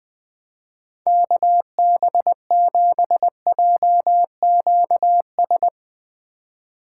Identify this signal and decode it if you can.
Morse code, heard as a keyed tone, about 20 wpm, 700 Hz. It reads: KB7JQS
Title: Speed Racing - USA Call Signs 17wpm